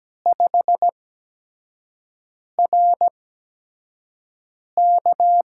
Morse code, heard as a keyed tone, about 17 wpm, spaced as for 5 wpm, 700 Hz. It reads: 5RK
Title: USA Call Signs - 5 Characters - 8x Character Spacing - 17wpm